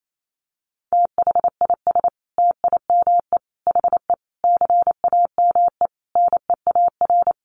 Morse code, heard as a keyed tone, about 28 wpm, 700 Hz. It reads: THIS TIME HE CAME NEAR
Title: Sentences from Top 1200 Words 28wpm